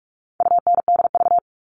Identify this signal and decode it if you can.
Morse code, heard as a keyed tone, about 44 wpm, 700 Hz. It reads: UNDV
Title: All Cut Numbers - 4-Digit 45wpm